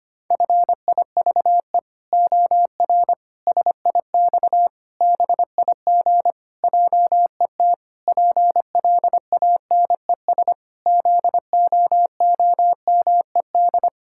Morse code, heard as a keyed tone, about 25 wpm, 700 Hz. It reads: FIVE OR SIX BIG JET PLANES ZOOMED
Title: Pangram Challenge v2 - 25wpm